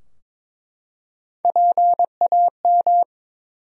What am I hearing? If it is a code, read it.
Morse code, heard as a keyed tone, about 21 wpm, 700 Hz. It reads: PAM
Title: USA Common Names - Max 3 Letters - 22wpm